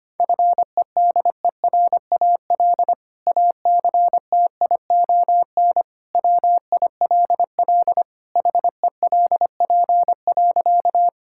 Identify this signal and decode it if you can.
Morse code, heard as a keyed tone, about 25 wpm, 700 Hz. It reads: FEDERAL ACTION WILL HELP.
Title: Sentences from Top 1000 Words - Review Part 5 - 25wpm